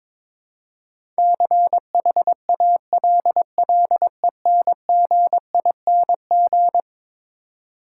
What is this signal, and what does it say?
Morse code, heard as a keyed tone, about 22 wpm, 700 Hz. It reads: CHALLENGING
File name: practice / 22wpm / Eleven Letter Words - 22wpm